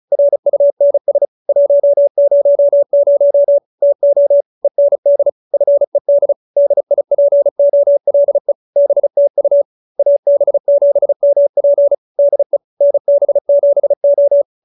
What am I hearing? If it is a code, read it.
Morse code, heard as a keyed tone, about 35 wpm, 570 Hz. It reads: RUNS 100 TO END FED DIPOLE BTU AB7MP DE NB7O